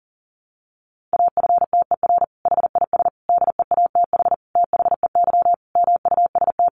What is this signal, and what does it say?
Morse code, heard as a keyed tone, about 40 wpm, 700 Hz. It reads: AFTER HIS DEATH THEY MUST
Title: Sentences from Top 1100 Words 40wpm